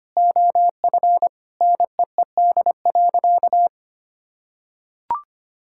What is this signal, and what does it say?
Morse code, heard as a keyed tone, about 25 wpm, 700 Hz. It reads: OF NEED.
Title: Sentences from Top 1000 Words - Review Part 5 - 25wpm